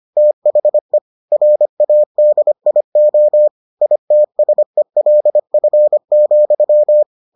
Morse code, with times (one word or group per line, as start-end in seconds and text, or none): 0.2-1.0 THE
1.3-3.5 RADIO
3.8-7.0 ITSELF,